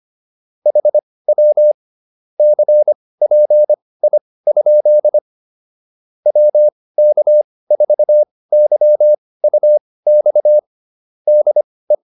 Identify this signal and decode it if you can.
Morse code, heard as a keyed tone, about 25 wpm, spaced as for 18 wpm, 600 Hz. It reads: HW CPI? WK4YUX DE